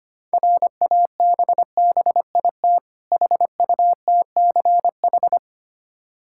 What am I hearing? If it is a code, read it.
Morse code, heard as a keyed tone, about 25 wpm, 700 Hz. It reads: RABBIT HUTCH
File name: practice / 25wpm / Difficult Words with Hints - 25wpm